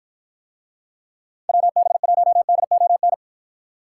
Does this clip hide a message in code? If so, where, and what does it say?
Yes: Morse code from 1.5 to 3.1 s, heard as a keyed tone, about 53 wpm, 700 Hz. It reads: WB1DON